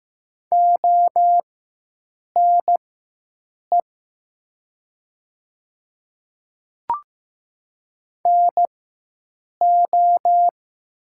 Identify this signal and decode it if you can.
Morse code, heard as a keyed tone, about 15 wpm, spaced as for 7 wpm, 700 Hz. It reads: ONE NO
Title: 4x Character Spacing - Top 300 Words - 15wpm